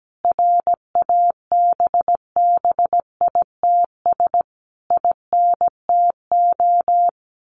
Morse code, heard as a keyed tone, about 17 wpm, 700 Hz. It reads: RABBITS INTO
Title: A Sound Of Thunder 17wpm